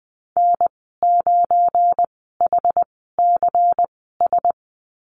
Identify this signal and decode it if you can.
Morse code, heard as a keyed tone, about 20 wpm, spaced as for 14 wpm, 700 Hz. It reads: N9HCS
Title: 173 Lesson 28g - Number 9 - Call Signs 20wpm